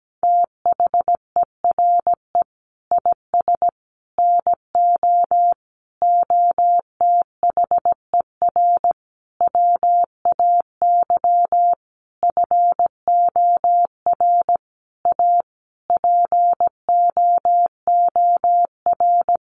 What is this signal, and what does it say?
Morse code, heard as a keyed tone, about 17 wpm, 700 Hz. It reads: THERE IS NO OTHER WAY FOR A POOR
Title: Sentences from Top 1300 Words 17wpm